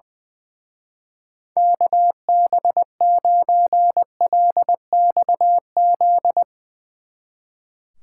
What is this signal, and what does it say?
Morse code, heard as a keyed tone, about 22 wpm, 700 Hz. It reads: KB9LXZ
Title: Speed Racing - USA Call Signs 20wpm